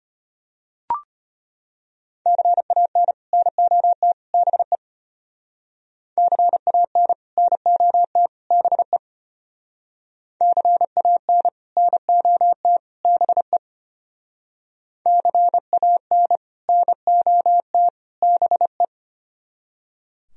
Morse code, heard as a keyed tone, about 30 wpm, 700 Hz. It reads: CAN NOT BE CAN NOT BE CAN NOT BE CAN NOT BE
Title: Speed-Racing - Sets of 3 Words 25wpm